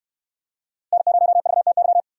Morse code, heard as a keyed tone, about 69 wpm, 700 Hz. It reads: N0VTY